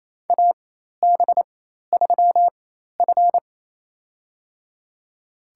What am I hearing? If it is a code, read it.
Morse code, heard as a keyed tone, about 27 wpm, spaced as for 13 wpm, 700 Hz. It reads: AB3F
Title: USA Call Signs - 4 Characters - 4x Character Spacing - 28wpm